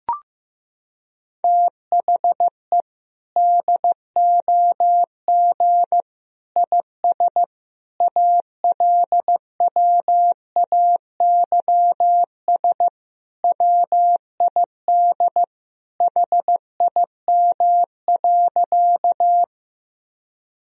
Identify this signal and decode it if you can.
Morse code, heard as a keyed tone, about 15 wpm, 700 Hz. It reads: THE DOG IS ALWAYS WID HIM.
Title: Sentences from Top 400 Words and Abbreviations 15wpm